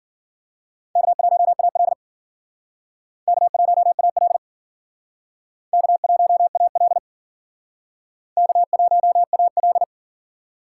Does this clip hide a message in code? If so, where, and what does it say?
Yes: Morse code from 1.0 to 9.8 s, heard as a keyed tone, about 52 wpm, 700 Hz. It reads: K1AL K1AL K1AL K1AL